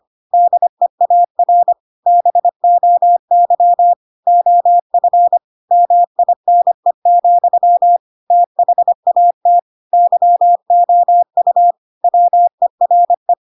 Morse code, heard as a keyed tone, about 25 wpm, 700 Hz. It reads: DEAR BOY OF MINE, THAT YOU WERE
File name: Pinocchio / Pinocchio ch7 25 25 ik1hge